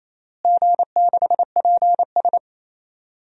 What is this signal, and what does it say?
Morse code, heard as a keyed tone, about 28 wpm, 700 Hz. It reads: G6PS
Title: DXCC-100 Call Signs - 4 Characters - Repeated 3x - 28wpm